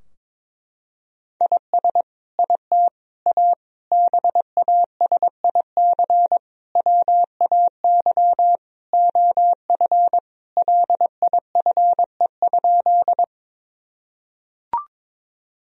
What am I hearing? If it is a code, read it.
Morse code, heard as a keyed tone, about 22 wpm, 700 Hz. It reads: IS IT A BASIC WAY OF LIFE?
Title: Sentences from Top 1000 Words - Review Part 3 - 22wpm